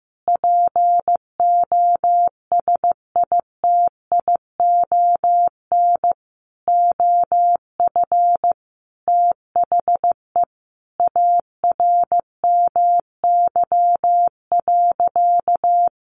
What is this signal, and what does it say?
Morse code, heard as a keyed tone, about 15 wpm, 700 Hz. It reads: POSITION OF THE ARMY.
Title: Sentences from Top 1400 Words 15wpm